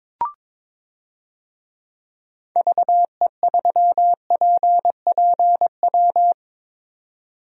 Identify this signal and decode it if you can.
Morse code, heard as a keyed tone, about 22 wpm, 700 Hz. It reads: VE3PPW
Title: DXCC-100 Call Signs - 6 Characters - 22wpm